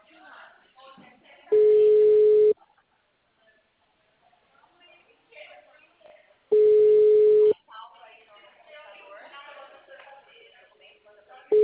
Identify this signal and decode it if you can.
Ringback tone (Europe, 425 Hz), with speech.